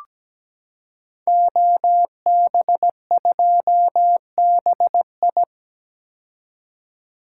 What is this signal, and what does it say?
Morse code, heard as a keyed tone, about 17 wpm, 700 Hz. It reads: OB2BI
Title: Code Groups - 5 Letter:Numbers - 17wpm